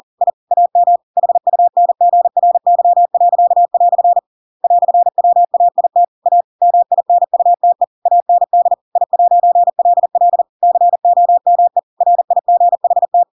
Morse code, heard as a keyed tone, about 40 wpm, 700 Hz. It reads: I AM HUNGRY." "WAIT A MINUTE AND I'LL COME RIGHT
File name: Pinocchio / Pinocchio ch6 40 40 ik1hge